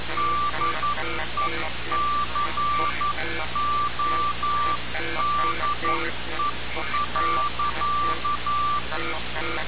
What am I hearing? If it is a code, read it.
Morse code, heard as a keyed tone, about 11 wpm, 1170 Hz. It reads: DECODE.
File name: radiocap / REC z